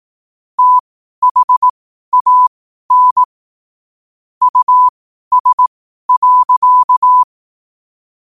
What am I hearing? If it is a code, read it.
Morse code, heard as a keyed tone, about 18 wpm, spaced as for 12 wpm, 1000 Hz. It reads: THAN US.
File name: qotdfiles / 12wpm